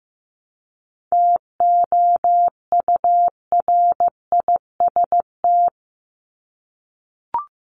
Morse code, heard as a keyed tone, about 15 wpm, 700 Hz. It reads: TOURIST
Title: Top 1501-1750 Words 15wpm